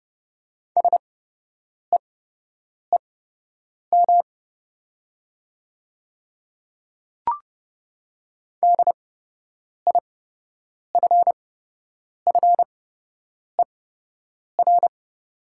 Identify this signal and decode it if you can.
Morse code, heard as a keyed tone, about 29 wpm, spaced as for 8 wpm, 690 Hz. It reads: SEEM DIFFER
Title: 8x Character Spacing - Top 500 Words - 30wpm